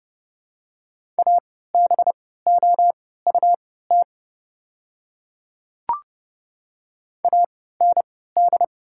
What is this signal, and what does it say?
Morse code, heard as a keyed tone, about 30 wpm, spaced as for 17 wpm, 700 Hz. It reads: ABOUT AND
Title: 3x Character Spacing - Top 200 Words - 30wpm